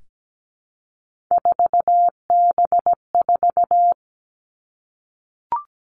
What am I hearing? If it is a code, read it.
Morse code, heard as a keyed tone, about 17 wpm, 700 Hz. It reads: 4B4
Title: Three Challenging Letter-Number Combinations 17wpm